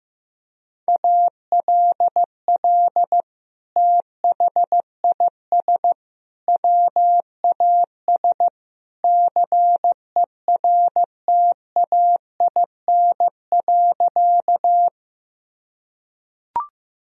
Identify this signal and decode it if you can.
Morse code, heard as a keyed tone, about 15 wpm, 700 Hz. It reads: ALL THIS WAS CERTAIN.
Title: Sentences from Top 1000 Words - Review Part 2 - 15wpm